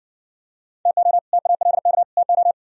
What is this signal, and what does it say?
Morse code, heard as a keyed tone, about 55 wpm, 700 Hz. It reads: TO TALK TO